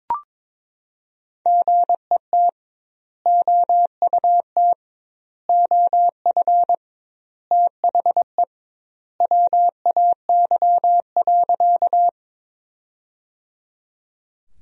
Morse code, heard as a keyed tone, about 22 wpm, 700 Hz. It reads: GET OUT OF THE WAY.
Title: 2x Word-Spacing - Sentences from Top 200 Words - 22wpm